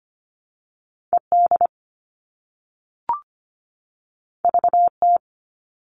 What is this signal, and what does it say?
Morse code, heard as a keyed tone, about 25 wpm, 700 Hz. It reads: ED VT
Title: All Cut Numbers - 2-Digit 25wpm